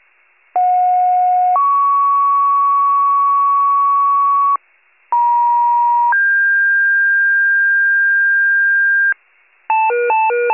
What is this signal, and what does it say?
Two-tone page, heard as a one-second tone, then a three-second tone, about 730 then 1120 Hz, about 950 then 1600 Hz, with noise. It is heard from 0.6 to 9.1 s.
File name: wcalert